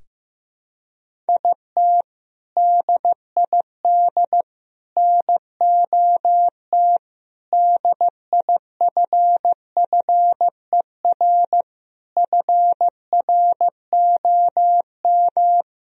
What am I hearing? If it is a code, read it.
Morse code, heard as a keyed tone, about 15 wpm, 700 Hz. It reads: IT DID NOT DIFFER FROM
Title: Sentences from Top 1000 Words - Review Part 1 - 15wpm